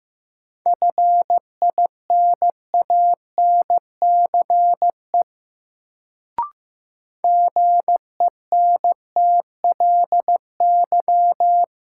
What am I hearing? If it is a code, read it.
Morse code, heard as a keyed tone, about 15 wpm, 700 Hz. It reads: FINANCE GENTLY
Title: Sentences from Top 1500 Words - New Word Drill - 15wpm